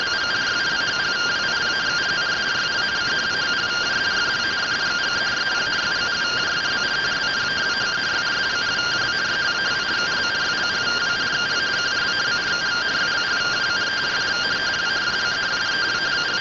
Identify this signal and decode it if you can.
Two-tone teleprinter signal, about 170 Hz shift.